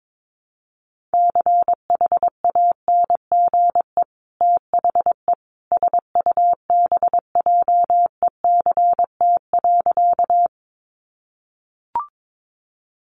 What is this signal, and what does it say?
Morse code, heard as a keyed tone, about 22 wpm, 700 Hz. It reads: CHANGE THE SUBJECT.
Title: Sentences from Top 800 Words 22wpm